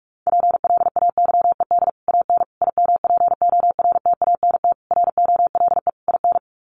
Morse code, heard as a keyed tone, about 44 wpm, 700 Hz. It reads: PLAYED AN IMPORTANT ROLE IN
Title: Sets of 5 Words Encore2 45wpm